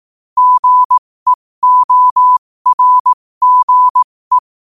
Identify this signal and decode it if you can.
Morse code, heard as a keyed tone, about 18 wpm, spaced as for 15 wpm, 1000 Hz. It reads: GEORGE
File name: qotdfiles / 15wpm